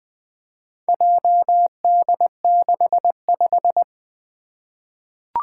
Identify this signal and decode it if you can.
Morse code, heard as a keyed tone, about 20 wpm, 700 Hz. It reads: JD65
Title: Four Challenging Letter-Number Combinations 20wpm